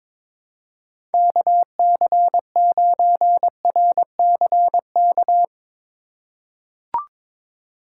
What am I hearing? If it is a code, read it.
Morse code, heard as a keyed tone, about 21 wpm, 700 Hz. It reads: KC9RCK
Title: USA Call Signs - 6 Characters - 2x Character Spacing - 22wpm